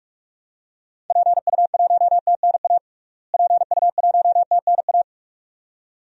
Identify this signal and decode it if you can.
Morse code, heard as a keyed tone, about 46 wpm, 700 Hz. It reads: PU1TNA PU1TNA
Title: DXCC-100 Call Signs - 6 Characters - Repeated 2x - 45wpm